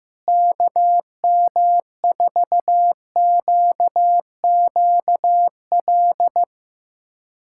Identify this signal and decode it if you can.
Morse code, heard as a keyed tone, about 16 wpm, 700 Hz. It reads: KM4QQL
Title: USA Callsign DE Callsign 15wpm